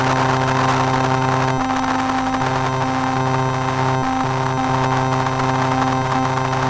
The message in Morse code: none